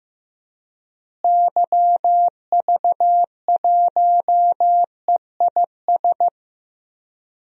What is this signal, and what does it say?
Morse code, heard as a keyed tone, about 15 wpm, 700 Hz. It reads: YV1EIS
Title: DXCC-100 Call Signs - 6 Characters - Repeated 3x - 15wpm